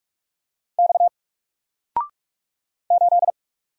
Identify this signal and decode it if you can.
Morse code, heard as a keyed tone, about 44 wpm, 700 Hz. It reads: X 8
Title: Single Letter-Number 45wpm